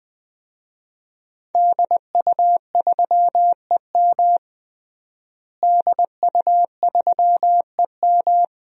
Morse code, heard as a keyed tone, about 20 wpm, 700 Hz. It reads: DU3EM DU3EM
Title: DXCC-100 Call Signs - 5 Characters - Repeated 3x - 20wpm